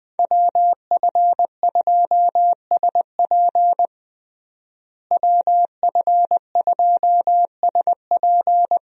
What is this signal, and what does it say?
Morse code, heard as a keyed tone, about 20 wpm, 700 Hz. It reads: WF2SP WF2SP
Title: USA Call Signs - 5 Characters - Repeated 3x - 20wpm